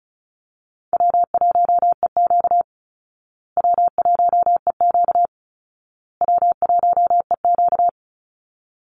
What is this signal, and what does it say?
Morse code, heard as a keyed tone, about 34 wpm, 700 Hz. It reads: W1EQ W1EQ W1EQ
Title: Easy Call Signs Repeated 3x - 35wpm